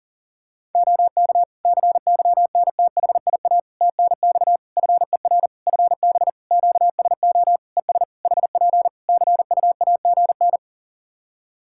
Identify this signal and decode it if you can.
Morse code, heard as a keyed tone, about 40 wpm, 700 Hz. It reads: OK CYNTHIA TNX FER FB QSO ES HP CUAGN
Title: K1USN SST QSO Practice 40wpm